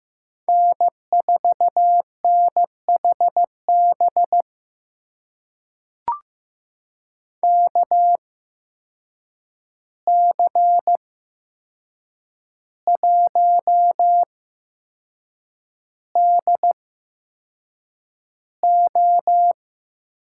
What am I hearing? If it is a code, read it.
Morse code, heard as a keyed tone, about 15 wpm, spaced as for 6 wpm, 700 Hz. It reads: N4NHB KC1DO
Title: USA Call Signs - 8x Character Spacing - 15wpm